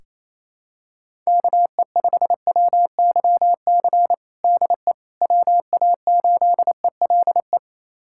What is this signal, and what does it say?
Morse code, heard as a keyed tone, about 28 wpm, 700 Hz. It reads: KE5WYC DE WA8ELE
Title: USA Callsign DE Callsign - 6 Characters - 28wpm